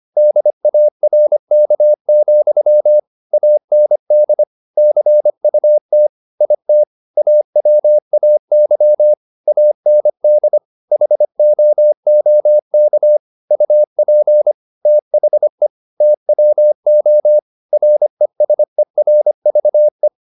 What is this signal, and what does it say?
Morse code, heard as a keyed tone, about 25 wpm, 600 Hz. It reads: DARK, AND CUT IT AWAY AND HOOK UP THE TWO RESERVE
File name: the old man and the sea part 1 25wpm